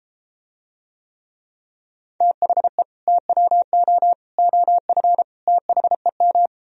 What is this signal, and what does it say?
Morse code, heard as a keyed tone, about 34 wpm, 700 Hz. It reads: THE TWO OF THEM